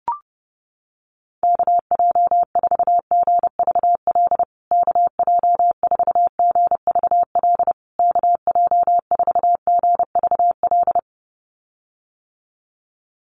Morse code, heard as a keyed tone, about 30 wpm, 700 Hz. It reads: KJ4GVL KJ4GVL KJ4GVL
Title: USA Call Signs Repeated 3x 30wpm